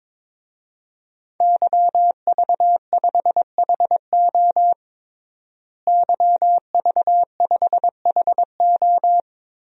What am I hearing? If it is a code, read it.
Morse code, heard as a keyed tone, about 22 wpm, 700 Hz. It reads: YV5HO YV5HO
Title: DXCC-100 Call Signs - 5 Characters - Repeated 2x - 22wpm